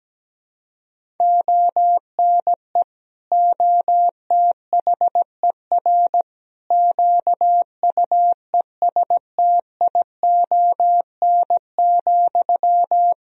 Morse code, heard as a keyed tone, about 17 wpm, 700 Hz. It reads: ONE OTHER QUESTION,
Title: Sentences from Top 400 Words and Abbreviations 17wpm